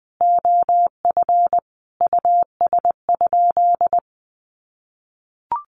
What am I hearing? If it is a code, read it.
Morse code, heard as a keyed tone, about 20 wpm, 700 Hz. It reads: OF US?
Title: Sentences from Top 600 Words 20wpm